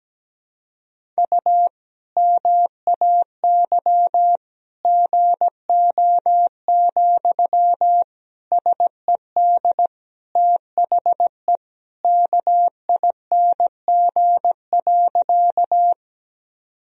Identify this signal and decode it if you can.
Morse code, heard as a keyed tone, about 17 wpm, 700 Hz. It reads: U MAY GO, SED THE KING.
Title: Sentences from Top 400 Words and Abbreviations 17wpm